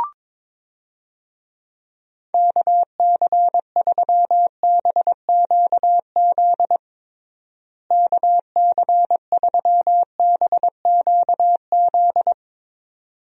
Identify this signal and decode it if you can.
Morse code, heard as a keyed tone, about 22 wpm, 700 Hz. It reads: KC3BQZ KC3BQZ
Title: USA Call Signs - 6 Characters - Repeated 2x - 22wpm